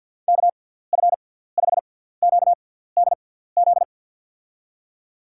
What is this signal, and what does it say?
Morse code, heard as a keyed tone, about 51 wpm, spaced as for 17 wpm, 700 Hz. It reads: KF5QDZ
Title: USA Call Signs - 6 Characters - 6x Character Spacing - 50wpm